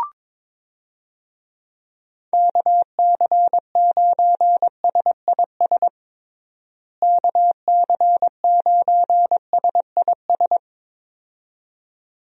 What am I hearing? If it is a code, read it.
Morse code, heard as a keyed tone, about 22 wpm, 700 Hz. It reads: KC9SIS KC9SIS
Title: USA Call Signs - 6 Characters - Repeated 2x - 22wpm